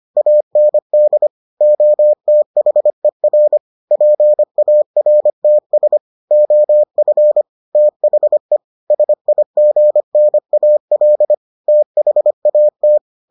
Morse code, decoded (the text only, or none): AND OTHER PARTS OF THE SIGNAL THAT